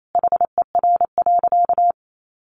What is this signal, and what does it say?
Morse code, heard as a keyed tone, about 28 wpm, 700 Hz. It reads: HER.
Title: Sentences from Top 1400 Words 28wpm